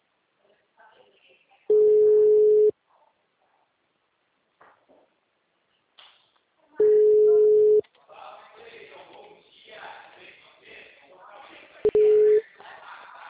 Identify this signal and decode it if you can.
Ringback tone (Europe, 425 Hz), with speech.